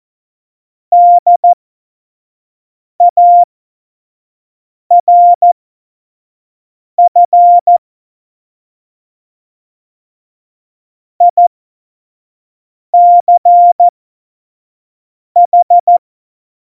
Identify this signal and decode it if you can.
Morse code, heard as a keyed tone, about 14 wpm, spaced as for 5 wpm, 700 Hz. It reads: DARF ICH
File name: Morsekurs / Witze / Witz 28. November 2017, 25 F70